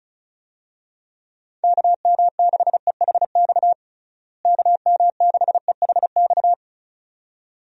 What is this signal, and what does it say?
Morse code, heard as a keyed tone, about 34 wpm, 700 Hz. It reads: KM6EHX KM6EHX
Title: USA Call Signs - 6 Characters - Repeated 2x - 35wpm